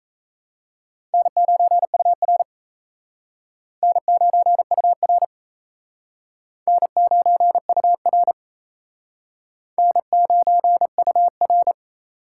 Morse code, heard as a keyed tone, about 34 wpm, 700 Hz. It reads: N9UR N9UR N9UR N9UR
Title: Speed Racing - USA 1x2 Call Signs 28wpm